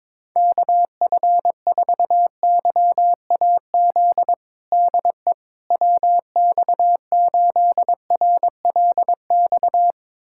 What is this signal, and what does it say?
Morse code, heard as a keyed tone, about 22 wpm, 700 Hz. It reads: KF4YAZ DE WX8RLX
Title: USA Callsign DE Callsign 22wpm